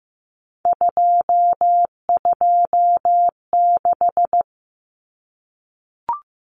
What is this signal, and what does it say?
Morse code, heard as a keyed tone, about 15 wpm, 700 Hz. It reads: 226
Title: Common Cut Numbers - 3-Digit 15wpm